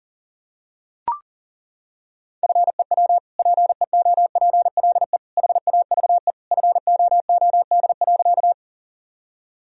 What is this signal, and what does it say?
Morse code, heard as a keyed tone, about 40 wpm, 700 Hz. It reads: FEW PEOPLE HAVE FOOD.
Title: Sentences from Top 1000 Words - Review Part 1 - 40wpm